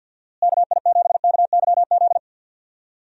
Morse code, heard as a keyed tone, about 46 wpm, 700 Hz. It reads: KI7KYZ
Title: USA Call Signs - 6 Characters - Repeated 2x - 50wpm